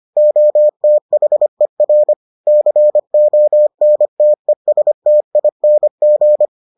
Morse code, heard as a keyed tone, about 25 wpm, 600 Hz. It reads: OTHER CONTESTING